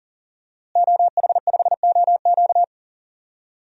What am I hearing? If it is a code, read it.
Morse code, heard as a keyed tone, about 40 wpm, 700 Hz. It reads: OH5OQ
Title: DXCC-100 Call Signs - 5 Characters - Repeated 3x - 40wpm